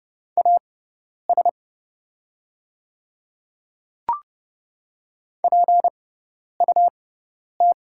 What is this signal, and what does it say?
Morse code, heard as a keyed tone, about 30 wpm, spaced as for 10 wpm, 700 Hz. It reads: AS PUT
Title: 6x Character Spacing - Top 200 Words - 30wpm